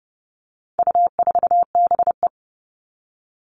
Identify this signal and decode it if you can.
Morse code, heard as a keyed tone, about 30 wpm, 700 Hz. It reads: U4BE
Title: All Cut Numbers - 4-Digit 30wpm